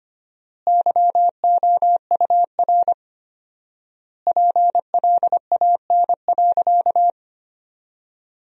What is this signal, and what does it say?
Morse code, heard as a keyed tone, about 25 wpm, 700 Hz. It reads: YOUR PLAN.
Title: 4x Word-Spacing - Sentences from Top 500 Words - 25wpm